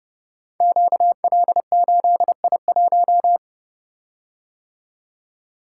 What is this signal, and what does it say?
Morse code, heard as a keyed tone, about 30 wpm, 700 Hz. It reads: QL8I1
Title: Code Groups - 5 Letter:Numbers - 30wpm